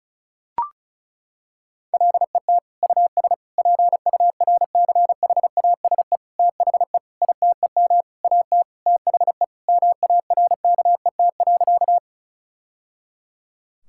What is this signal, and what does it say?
Morse code, heard as a keyed tone, about 36 wpm, 700 Hz. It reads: LET US PURCHASE THE ITEM AT THE MARKET.
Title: Sentences from Top 1000 Words - Review Part 6 - 35wpm